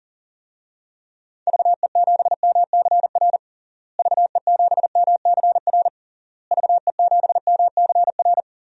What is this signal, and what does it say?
Morse code, heard as a keyed tone, about 40 wpm, 700 Hz. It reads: VE7MCR VE7MCR VE7MCR
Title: DXCC-100 Call Signs - 6 Characters - Repeated 3x - 40wpm